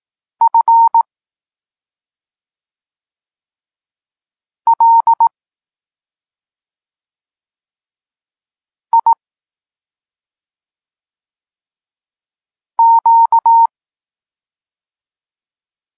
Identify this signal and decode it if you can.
Morse code, heard as a keyed tone, about 18 wpm, 920 Hz. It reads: F L I Q
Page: Tekst 32 – Doskonalenie odbioru poznanych liter